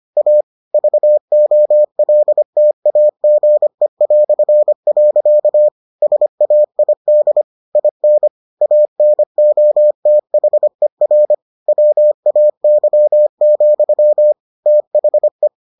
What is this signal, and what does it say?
Morse code, heard as a keyed tone, about 25 wpm, 600 Hz. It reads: A VOLTAGE". SAID IN ANOTHER WAY, THE